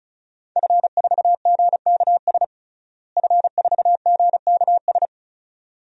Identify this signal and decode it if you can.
Morse code, heard as a keyed tone, about 36 wpm, 700 Hz. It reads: F4GKS F4GKS
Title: DXCC-100 Call Signs - 5 Characters - Repeated 2x - 35wpm